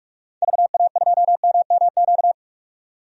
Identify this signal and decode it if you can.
Morse code, heard as a keyed tone, about 46 wpm, 700 Hz. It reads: VA2MMQ
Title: DXCC-100 Call Signs - 6 Characters - 45wpm